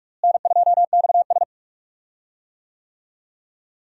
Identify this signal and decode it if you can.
Morse code, heard as a keyed tone, about 47 wpm, 700 Hz. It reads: N2XS
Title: USA Call Signs - 4 Characters - Repeated 2x - 45wpm